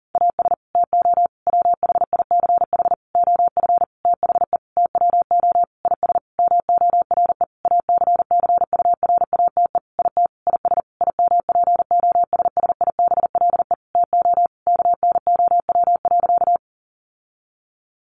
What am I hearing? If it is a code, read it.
Morse code, heard as a keyed tone, about 40 wpm, 700 Hz. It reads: AS TO WHICH OF THE TWO IS MORE ACCURATE IT IS IMPOSSIBLE TO KNOW.
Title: Sentences from Top 1200 Words 40wpm